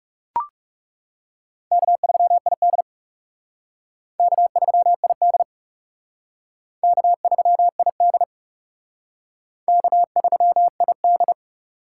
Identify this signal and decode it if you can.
Morse code, heard as a keyed tone, about 35 wpm, 700 Hz. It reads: K3ID K3ID K3ID K3ID
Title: Speed Racing - USA 1x2 Call Signs 30wpm